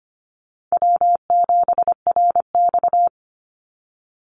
Morse code, heard as a keyed tone, about 25 wpm, 700 Hz. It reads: W7RX
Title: Easy Call Signs Repeated 3x - 25wpm